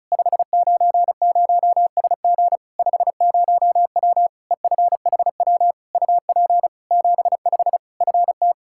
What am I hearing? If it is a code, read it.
Morse code, heard as a keyed tone, about 36 wpm, 700 Hz. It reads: 590SG 50W EFHW UP 75 FT
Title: Practice QSO - Exchange 2 - Info - 35wpm